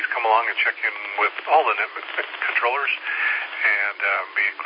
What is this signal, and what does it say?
Emergency Alert System (SAME) tones, heard as harsh buzzing bursts, with speech. It is heard from 0.5 to 4.1 s.